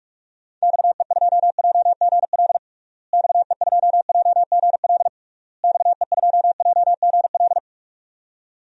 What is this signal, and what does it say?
Morse code, heard as a keyed tone, about 46 wpm, 700 Hz. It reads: XE2JGL XE2JGL XE2JGL
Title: DXCC-100 Call Signs - 6 Characters - Repeated 3x - 45wpm